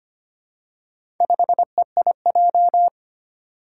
Morse code, heard as a keyed tone, about 25 wpm, 700 Hz. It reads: 5EIJ
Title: Four Challenging Letter-Number Combinations 25wpm